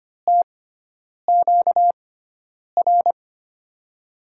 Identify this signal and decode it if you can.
Morse code, heard as a keyed tone, about 25 wpm, spaced as for 9 wpm, 700 Hz. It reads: TQR
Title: USA Call Signs - 6 Characters - 6x Character Spacing - 25wpm